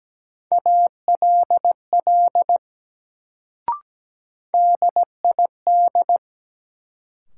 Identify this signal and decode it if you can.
Morse code, heard as a keyed tone, about 17 wpm, 700 Hz. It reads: ALL DID
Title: Sentences from Top 100 Words - New Word Drill - 17wpm